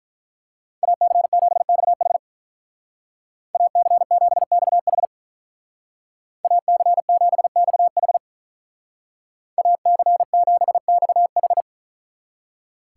Morse code, heard as a keyed tone, about 44 wpm, 700 Hz. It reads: AC7XH AC7XH AC7XH AC7XH
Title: Speed Racing - USA 2x2 Call Signs 35wpm